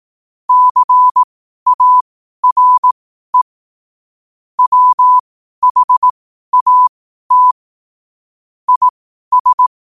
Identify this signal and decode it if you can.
Morse code, heard as a keyed tone, about 18 wpm, spaced as for 12 wpm, 1000 Hz. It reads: CARE WHAT IS
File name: qotdfiles / 12wpm